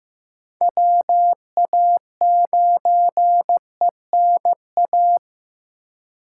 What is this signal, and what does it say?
Morse code, heard as a keyed tone, about 15 wpm, 700 Hz. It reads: WA9ENA
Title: K1USN SST QSO Practice - 15wpm